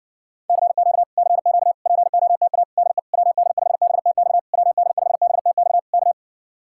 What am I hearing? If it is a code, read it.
Morse code, heard as a keyed tone, about 60 wpm, 700 Hz. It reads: CQ CQ POTA DE WD5BTX WD5BTX K